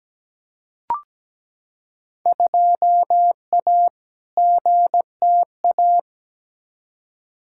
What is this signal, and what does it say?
Morse code, heard as a keyed tone, about 17 wpm, 700 Hz. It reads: 2A GTA
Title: Field Day Exchanges Encore - 17wpm